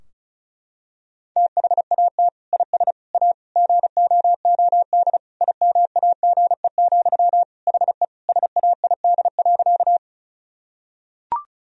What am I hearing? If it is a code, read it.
Morse code, heard as a keyed tone, about 36 wpm, 700 Hz. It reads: THAT IS A GOOD IMAGE, HE SAID.
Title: Sentences from Top 1000 Words - Review Part 4 - 35wpm